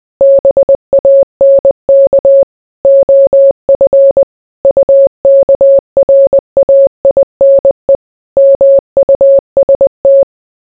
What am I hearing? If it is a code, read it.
Morse code, heard as a keyed tone, about 20 wpm, 550 Hz. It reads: BANK OF UKRAINE MUST